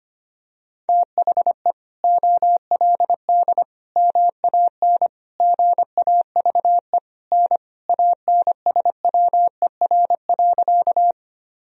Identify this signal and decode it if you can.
Morse code, heard as a keyed tone, about 25 wpm, 700 Hz. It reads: THE OLD MAN GAVE N ANSWER.
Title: Sentences from Top 500 Words and Abbreviations 25wpm